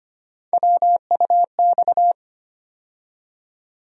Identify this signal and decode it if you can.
Morse code, heard as a keyed tone, about 25 wpm, 700 Hz. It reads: WUX
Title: USA Call Sign Postfixes - 3 Letters 25wpm